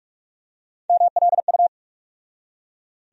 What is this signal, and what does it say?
Morse code, heard as a keyed tone, about 41 wpm, 700 Hz. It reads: MLU
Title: USA Call Sign Postfixes - 3 Letters 45wpm